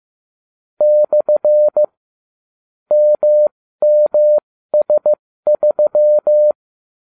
Morse code, heard as a keyed tone, about 15 wpm, spaced as for 13 wpm, 600 Hz. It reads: / MMS3
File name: cw / ubacw30 3